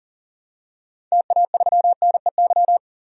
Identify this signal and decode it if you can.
Morse code, heard as a keyed tone, about 40 wpm, 700 Hz. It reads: TA3NEY
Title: DXCC-100 Call Signs - 6 Characters - 40wpm